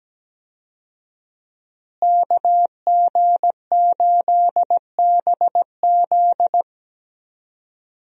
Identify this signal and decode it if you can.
Morse code, heard as a keyed tone, about 17 wpm, 700 Hz. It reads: KG8BZ
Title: USA Call Signs - 5 Characters - Repeated 2x - 17wpm